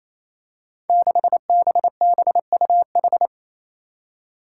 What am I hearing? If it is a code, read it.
Morse code, heard as a keyed tone, about 28 wpm, 700 Hz. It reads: 6BBUH
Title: Five Challenging Letter-Number Combinations 28wpm